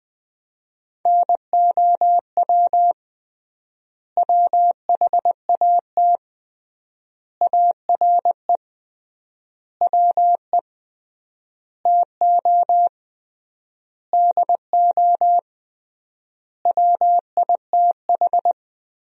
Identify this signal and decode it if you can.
Morse code, heard as a keyed tone, about 21 wpm, 700 Hz. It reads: NOW WHAT ARE WE TO DO WITH
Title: 3x Word-Spacing - Sentences from Top 100 Words - 20wpm